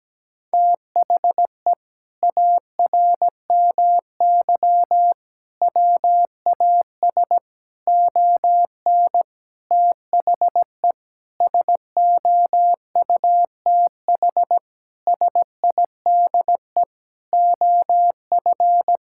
Morse code, heard as a keyed tone, about 17 wpm, 700 Hz. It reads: THE ARMY WAS ON THE SOUTH SIDE OF
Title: Sentences from Top 1000 Words - Review Part 5 - 17wpm